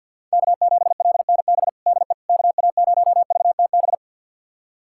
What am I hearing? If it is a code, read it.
Morse code, heard as a keyed tone, about 46 wpm, 700 Hz. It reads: K7LNB DE KA0VTB
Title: USA Callsign DE Callsign 50wpm